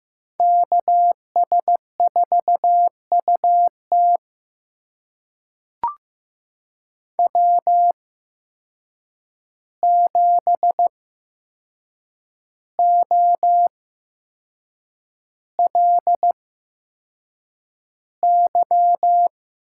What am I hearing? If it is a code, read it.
Morse code, heard as a keyed tone, about 15 wpm, spaced as for 6 wpm, 700 Hz. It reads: KS4UT W7OLY
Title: USA Call Signs - 8x Character Spacing - 15wpm